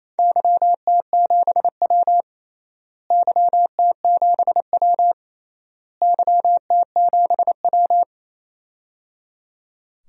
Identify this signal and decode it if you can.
Morse code, heard as a keyed tone, about 28 wpm, 700 Hz. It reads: YT7W YT7W YT7W
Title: DXCC-100 Call Signs - 4 Characters - Repeated 3x - 28wpm